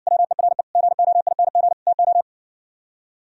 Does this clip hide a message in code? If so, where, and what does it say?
Yes: Morse code from 0.1 to 2.2 s, heard as a keyed tone, about 60 wpm, 700 Hz. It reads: WERE GOING TO